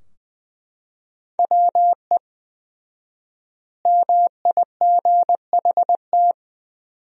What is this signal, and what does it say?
Morse code, heard as a keyed tone, about 20 wpm, 700 Hz. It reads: WE MIGHT
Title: 4x Word-Spacing - Sentences from Top 300 Words - 20wpm